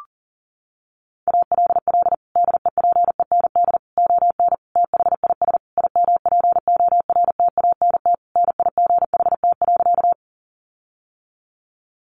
Morse code, heard as a keyed tone, about 40 wpm, 700 Hz. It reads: ALL DEPEND ON THIS IMPORTANT NIGHT.
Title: Sentences from Top 1300 Words 40wpm